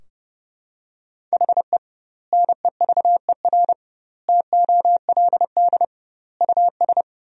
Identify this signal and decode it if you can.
Morse code, heard as a keyed tone, about 32 wpm, 700 Hz. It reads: HE NEVER TOLD US
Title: 2x Word-Spacing - Sentences from Top 400 Words - 30wpm